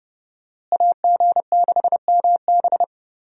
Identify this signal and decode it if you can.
Morse code, heard as a keyed tone, about 30 wpm, 700 Hz. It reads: AG6MB
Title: USA Call Signs - 5 Characters - Repeated 2x - 30wpm